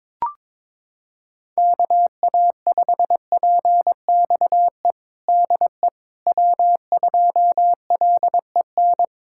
Morse code, heard as a keyed tone, about 22 wpm, 700 Hz. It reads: KA5PXE DE W2LEN
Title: USA Callsign DE Callsign 22wpm